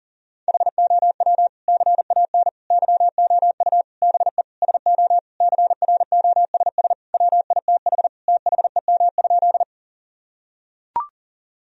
Morse code, heard as a keyed tone, about 37 wpm, 700 Hz. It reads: HOW CAN YOU BE SO CROSS WITH THEM?
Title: Sentences from Top 1000 Words - Review Part 1 - 40wpm